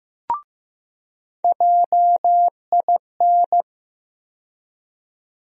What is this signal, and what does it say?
Morse code, heard as a keyed tone, about 15 wpm, 700 Hz. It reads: JIN
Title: USA Call Sign Postfixes - 3 Letters 15wpm